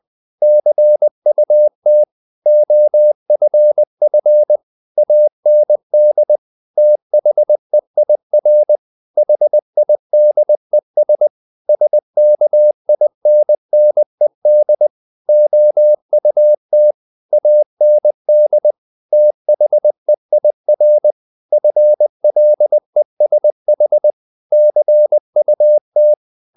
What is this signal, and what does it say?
Morse code, heard as a keyed tone, about 20 wpm, 600 Hz. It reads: CUT OFF AND THEIR HIDES SKINNED OUT AND THEIR FLESH CUT
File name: the old man and the sea / the old man and the sea part 1 20wpm